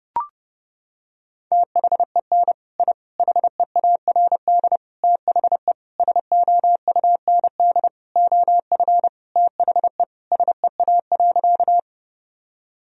Morse code, heard as a keyed tone, about 30 wpm, 700 Hz. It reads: THEN I HEARD THE SOUND OF THE SEA.
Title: Sentences from Top 1000 Words - Review Part 1 - 30wpm